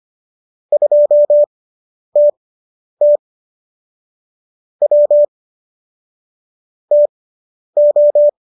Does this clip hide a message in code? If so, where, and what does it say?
Yes: Morse code from 0.7 to 8.3 s, heard as a keyed tone, about 25 wpm, spaced as for 10 wpm, 590 Hz. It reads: 2TT W TO